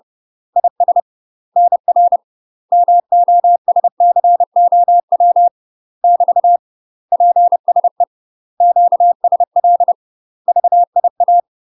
Morse code, heard as a keyed tone, about 30 wpm, 700 Hz. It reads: IS NR MOSCOW <BT> PSE QSL VIA
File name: UK MASTER QSOs TRAINING / 30WPM / UK MASTER QSOs 0002